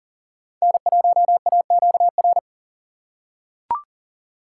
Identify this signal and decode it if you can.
Morse code, heard as a keyed tone, about 40 wpm, 700 Hz. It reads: N1AQR
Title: USA Call Signs - 5 Characters - 4x Character Spacing - 40wpm